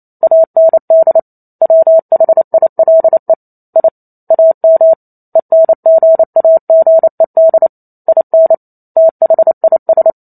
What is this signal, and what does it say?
Morse code, heard as a keyed tone, about 28 wpm, 650 Hz. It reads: AND WHILE I AM ENGAGED IN THIS